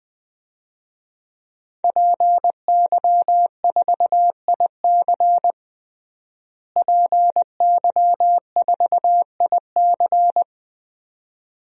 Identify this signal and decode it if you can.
Morse code, heard as a keyed tone, about 21 wpm, 700 Hz. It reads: PY4IC PY4IC
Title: DXCC-100 Call Signs - 5 Characters - Repeated 2x - 20wpm